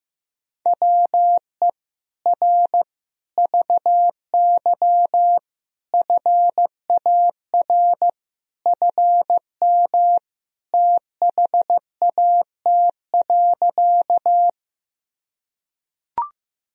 Morse code, heard as a keyed tone, about 15 wpm, 700 Hz. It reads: WE R VY FAR FM THAT.
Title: Sentences from Top 300 Words and Abbreviations 15wpm